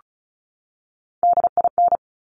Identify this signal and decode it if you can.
Morse code, heard as a keyed tone, about 36 wpm, 700 Hz. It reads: DIN
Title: Top Three Letter Combinations in Words 35wpm